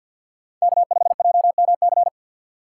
Morse code, heard as a keyed tone, about 50 wpm, 700 Hz. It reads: K5JMC
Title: USA Call Signs - 2x Character Spacing - 50wpm